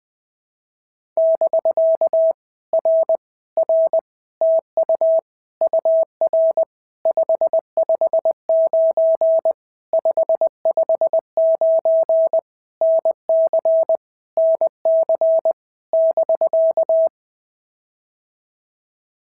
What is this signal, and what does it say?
Morse code, heard as a keyed tone, about 20 wpm, 650 Hz. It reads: <BK> R R TU UR 559 559 NC NC <BK>
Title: POTA QSO Practice - 20wpm